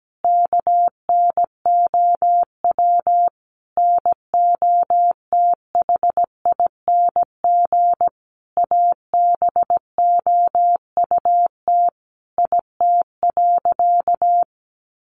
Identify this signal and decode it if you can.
Morse code, heard as a keyed tone, about 17 wpm, 700 Hz. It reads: KNOW NOTHING ABOUT IT.
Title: Sentences from Top 500 Words 17wpm